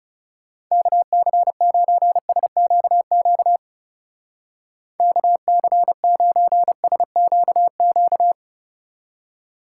Morse code, heard as a keyed tone, about 37 wpm, 700 Hz. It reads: KC9SQQ KC9SQQ
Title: Speed Racing - USA 2x3 Call Signs 30wpm